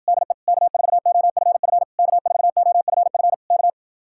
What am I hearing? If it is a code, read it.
Morse code, heard as a keyed tone, about 52 wpm, 700 Hz. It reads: DE K4OFF K4OFF K